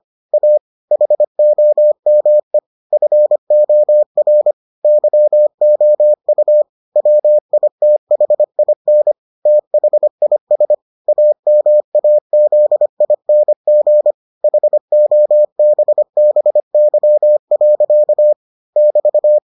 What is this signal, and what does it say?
Morse code, heard as a keyed tone, about 25 wpm, 600 Hz. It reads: A HOME FOR YOU WITHIN THIS AMAZING HOBBY. <BT>